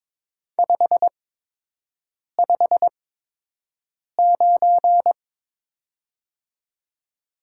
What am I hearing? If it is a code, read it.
Morse code, heard as a keyed tone, about 22 wpm, spaced as for 6 wpm, 700 Hz. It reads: 559
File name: Top 90 QSO Elements - 8x Character Spacing - 22wpm